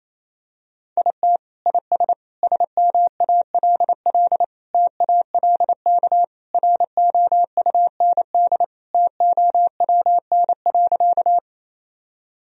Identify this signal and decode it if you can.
Morse code, heard as a keyed tone, about 28 wpm, 700 Hz. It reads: IT IS SMALL TALK ROUND TOWN.
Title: Sentences from Top 1000 Words - Review Part 1 - 28wpm